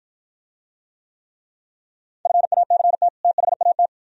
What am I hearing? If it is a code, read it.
Morse code, heard as a keyed tone, about 53 wpm, 700 Hz. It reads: FACT THAT